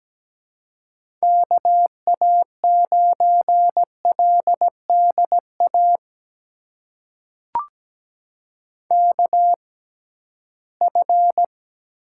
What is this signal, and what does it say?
Morse code, heard as a keyed tone, about 17 wpm, spaced as for 13 wpm, 700 Hz. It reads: KA9LDA KF
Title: USA Call Signs - 6 Characters - 6x Character Spacing - 17wpm